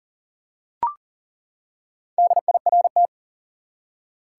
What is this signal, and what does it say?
Morse code, heard as a keyed tone, about 40 wpm, 700 Hz. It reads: DIRT
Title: Four Letter Words - 40wpm